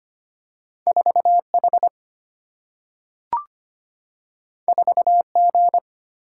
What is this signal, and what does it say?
Morse code, heard as a keyed tone, about 25 wpm, 700 Hz. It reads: 4H 4G
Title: Single-Letter Single-Number 25wpm